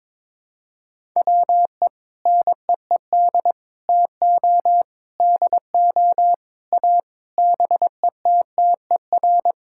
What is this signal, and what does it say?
Morse code, heard as a keyed tone, about 22 wpm, 700 Hz. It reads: WE NEED TO DO A BETTER
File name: Sentences with 7-Word Phrases - 22wpm